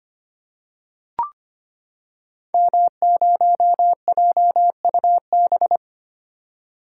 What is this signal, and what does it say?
Morse code, heard as a keyed tone, about 25 wpm, 700 Hz. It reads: M0JUB
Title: Code Groups - 5 Letter:Numbers - 25wpm